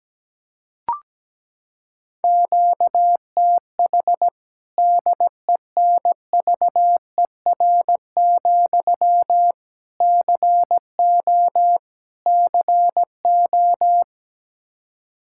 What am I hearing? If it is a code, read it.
Morse code, heard as a keyed tone, about 17 wpm, 700 Hz. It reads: QTH DENVER, CO CO
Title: K1USN SST QSO Practice 17wpm